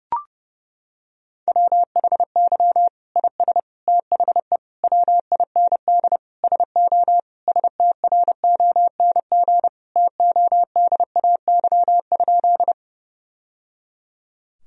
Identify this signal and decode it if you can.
Morse code, heard as a keyed tone, about 30 wpm, 700 Hz. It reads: WHY IS THE WIND SO STRONG TODAY?
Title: Sentences from Top 1000 Words - Review Part 2 - 30wpm